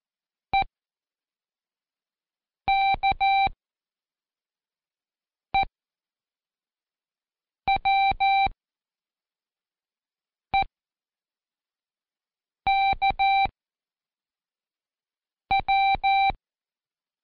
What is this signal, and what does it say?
Morse code, heard as a keyed tone, about 14 wpm, 780 Hz. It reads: E K E W E K W
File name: Code Quick / AudioDownld1 / Seg07-Day1-Pract EWK